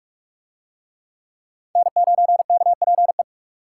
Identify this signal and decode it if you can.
Morse code, heard as a keyed tone, about 42 wpm, 700 Hz. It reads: N9KPE